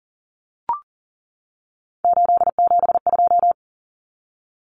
Morse code, heard as a keyed tone, about 37 wpm, 700 Hz. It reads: 872
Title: Common Cut Numbers - 3-Digit 40wpm